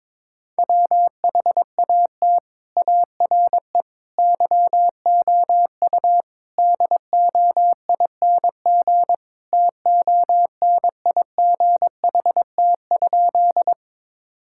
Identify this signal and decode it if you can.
Morse code, heard as a keyed tone, about 22 wpm, 700 Hz. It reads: WHAT ARE YOU DOING TONIGHT?
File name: practice / 22wpm / Sentences with 4-Word Phrases - 22wpm